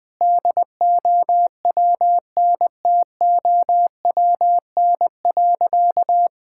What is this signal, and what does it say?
Morse code, heard as a keyed tone, about 21 wpm, 700 Hz. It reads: DOWNTOWN.
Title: Sentences with 6-Word Phrases - 20wpm